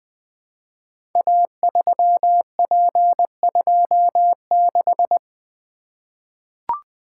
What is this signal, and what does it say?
Morse code, heard as a keyed tone, about 20 wpm, 700 Hz. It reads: A3P26
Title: Code Groups - 5 Letter:Numbers - 20wpm